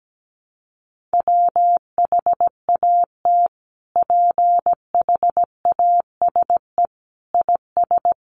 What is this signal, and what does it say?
Morse code, heard as a keyed tone, about 17 wpm, 700 Hz. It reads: WHAT PHASE IS
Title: Sentences from Top 1200 Words 17wpm